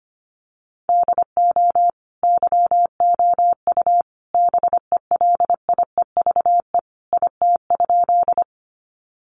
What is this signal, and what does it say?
Morse code, heard as a keyed tone, about 25 wpm, 700 Hz. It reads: DO YOU BELIEVE IT?
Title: Sentences from Top 1000 Words 25wpm